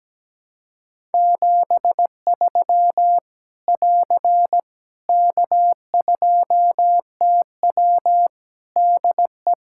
Morse code, heard as a keyed tone, about 17 wpm, 700 Hz. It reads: 73 <AR> K2TW DE
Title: Practice QSO - Exchange 4 - Ending - 17wpm